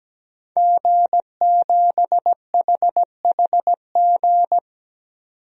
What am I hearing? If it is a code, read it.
Morse code, heard as a keyed tone, about 17 wpm, 700 Hz. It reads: G7HHG
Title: DXCC-100 Call Signs - 5 Characters - 17wpm